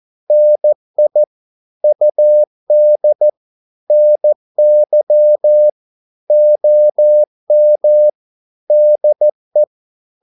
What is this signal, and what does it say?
Morse code, heard as a keyed tone, about 14 wpm, 600 Hz. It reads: NI UD NY OM DE
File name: Korte ord DK 14wpm